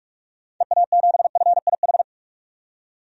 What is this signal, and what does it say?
Morse code, heard as a keyed tone, about 47 wpm, 700 Hz. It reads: EA7FIH
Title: DXCC-100 Call Signs - 6 Characters - 45wpm